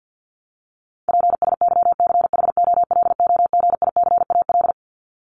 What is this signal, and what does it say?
Morse code, heard as a keyed tone, about 50 wpm, 700 Hz. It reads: PSYCHOLOGICAL
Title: Top 1251-1500 Words 50wpm